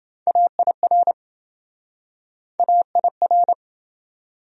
Morse code, heard as a keyed tone, about 27 wpm, 700 Hz. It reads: AIR AIR